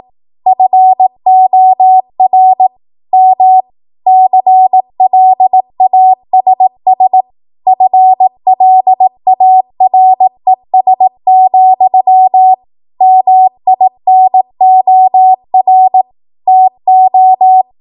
Morse code, heard as a keyed tone, about 18 wpm, 750 Hz. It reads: FOR M CLASS FLARES, MINOR TO
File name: Morse / 18WPM